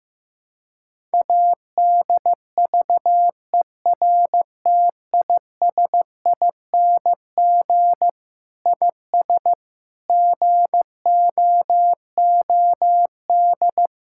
Morse code, heard as a keyed tone, about 15 wpm, 700 Hz. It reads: ADVERTISING IS GOOD
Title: Sentences from Top 1000 Words - Review Part 6 - 15wpm